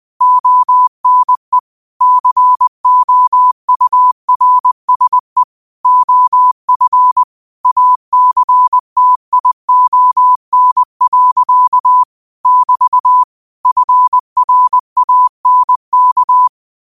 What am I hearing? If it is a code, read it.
Morse code, heard as a keyed tone, about 20 wpm, 1000 Hz. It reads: ONE COURSE OF ACTION. <BT> FRANK
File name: qotdfiles / 20wpm